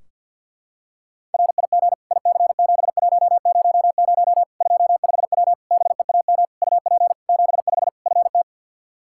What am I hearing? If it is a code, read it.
Morse code, heard as a keyed tone, about 47 wpm, 700 Hz. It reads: RIG IC7100 25W BEAM UP 75 FT
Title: Practice QSO - Exchange 2 - Info - 50wpm